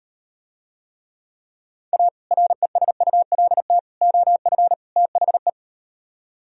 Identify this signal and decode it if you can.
Morse code, heard as a keyed tone, about 38 wpm, 700 Hz. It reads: A RESULT OF THE